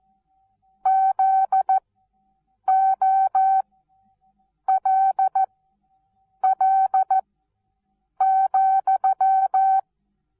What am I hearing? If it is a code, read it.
Morse code, heard as a keyed tone, about 14 wpm, spaced as for 7 wpm, 770 Hz. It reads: ZOLL,
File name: Onlinelehrgang / morsen / DJ4UF Morsen19